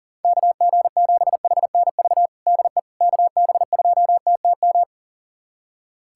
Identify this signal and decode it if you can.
Morse code, heard as a keyed tone, about 40 wpm, 700 Hz. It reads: KG7HNV DE KB2TTM
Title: USA Callsign DE Callsign - 6 Characters - 40wpm